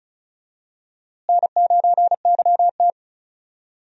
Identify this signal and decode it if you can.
Morse code, heard as a keyed tone, about 36 wpm, 700 Hz. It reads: N9YT
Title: USA Call Signs - 4 Characters - Repeated 2x - 35wpm